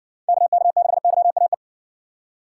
Morse code, heard as a keyed tone, about 54 wpm, 700 Hz. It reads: KK6YAE